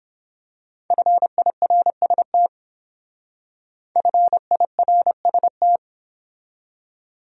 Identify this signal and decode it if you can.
Morse code, heard as a keyed tone, about 26 wpm, 700 Hz. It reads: FIRST FIRST
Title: Speed-Racing Top 200 Words 22wpm